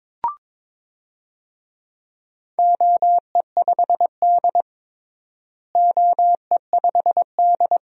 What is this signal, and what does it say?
Morse code, heard as a keyed tone, about 22 wpm, 700 Hz. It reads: OE5D OE5D
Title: DXCC-100 Call Signs - 4 Characters - Repeated 3x - 22wpm